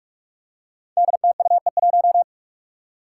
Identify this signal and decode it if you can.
Morse code, heard as a keyed tone, about 46 wpm, 700 Hz. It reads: DTUE1
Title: Five Challenging Letter-Number Combinations 45wpm